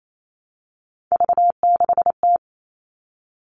Morse code, heard as a keyed tone, about 28 wpm, 700 Hz. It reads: V6T
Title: All Cut Numbers - 3-Digit 28wpm